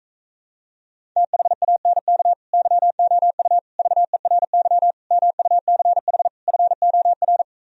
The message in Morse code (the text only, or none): THANK YOU VERY MUCH FOR